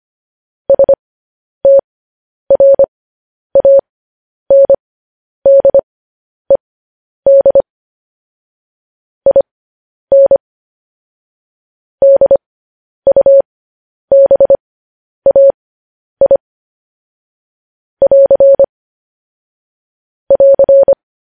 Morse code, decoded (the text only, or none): STRANDED IN DUBAI <AR> <AR>